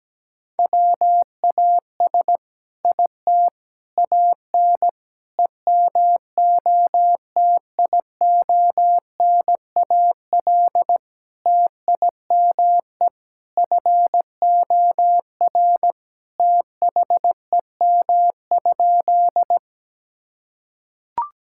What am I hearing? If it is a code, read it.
Morse code, heard as a keyed tone, about 17 wpm, 700 Hz. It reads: WAS IT AN EMOTIONAL TIME FOR THEM?
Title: Sentences from Top 1000 Words - Review Part 5 - 17wpm